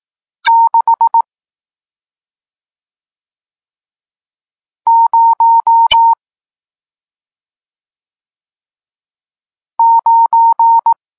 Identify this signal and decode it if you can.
Morse code, heard as a keyed tone, about 18 wpm, 920 Hz. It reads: 6 0 9
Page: Tekst 40 – Nauka znaków przestankowych